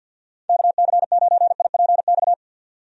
Morse code, heard as a keyed tone, about 51 wpm, 700 Hz. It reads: KC9IPX
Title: USA Call Signs - 6 Characters - 50wpm